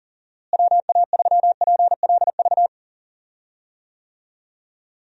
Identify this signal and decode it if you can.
Morse code, heard as a keyed tone, about 40 wpm, 700 Hz. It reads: WA3PLV
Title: USA Call Signs - 6 Characters - Repeated 2x - 40wpm